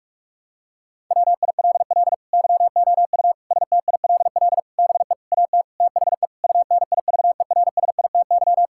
Morse code, heard as a keyed tone, about 46 wpm, 700 Hz. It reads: WILL YOU STILL BE AT THE UNIVERSITY
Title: Sentences from Top 1000 Words - Review Part 5 - 45wpm